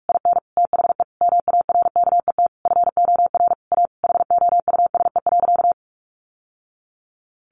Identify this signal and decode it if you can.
Morse code, heard as a keyed tone, about 44 wpm, 700 Hz. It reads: IN THE MARKET FOR A HOUSE.
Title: Sentences from Top 600 Words 45wpm